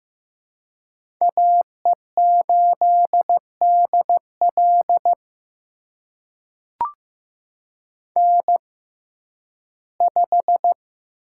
Morse code, heard as a keyed tone, about 15 wpm, spaced as for 11 wpm, 700 Hz. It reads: AE8DL N5
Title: USA Call Signs - 5 Characters - 6x Character Spacing - 15wpm